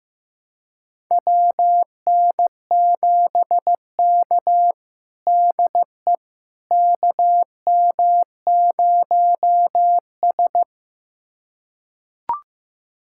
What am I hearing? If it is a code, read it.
Morse code, heard as a keyed tone, about 16 wpm, 700 Hz. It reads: WN7K DE KM0S
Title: USA Callsign DE Callsign - 4 Characters - 15wpm